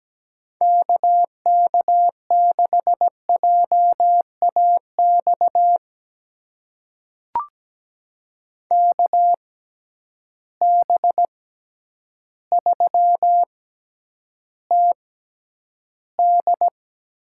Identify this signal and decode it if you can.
Morse code, heard as a keyed tone, about 17 wpm, spaced as for 9 wpm, 700 Hz. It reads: KK6JAX KB3TD
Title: USA Call Signs - 6 Characters - 6x Character Spacing - 17wpm